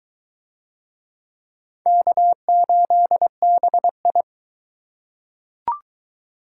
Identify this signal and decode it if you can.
Morse code, heard as a keyed tone, about 22 wpm, 700 Hz. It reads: K8BI